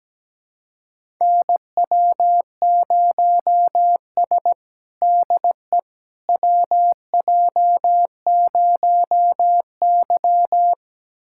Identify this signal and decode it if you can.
Morse code, heard as a keyed tone, about 17 wpm, 700 Hz. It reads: NW0S DE WJ0Y
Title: USA Callsign DE Callsign - 4 Characters - 17wpm